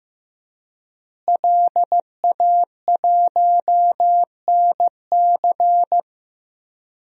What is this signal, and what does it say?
Morse code, heard as a keyed tone, about 15 wpm, 700 Hz. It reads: LA1NC
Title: DXCC-100 Call Signs - 5 Characters - Repeated 2x - 15wpm